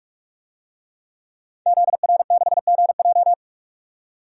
Morse code, heard as a keyed tone, about 41 wpm, 700 Hz. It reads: ZR6GJ
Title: DXCC-100 Call Signs - 5 Characters - 45wpm